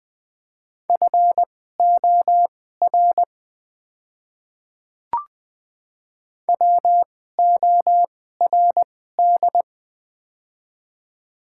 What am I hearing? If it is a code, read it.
Morse code, heard as a keyed tone, about 21 wpm, spaced as for 15 wpm, 700 Hz. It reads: FOR WORD
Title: 2x Character Spacing - Top 100 Words - 20wpm